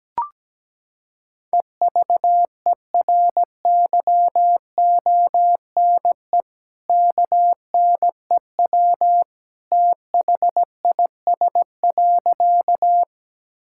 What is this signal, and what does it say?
Morse code, heard as a keyed tone, about 17 wpm, 700 Hz. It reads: EVERYONE KNEW THIS.
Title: Sentences from Top 1000 Words - Review Part 5 - 17wpm